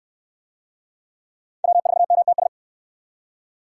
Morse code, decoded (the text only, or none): W4MTS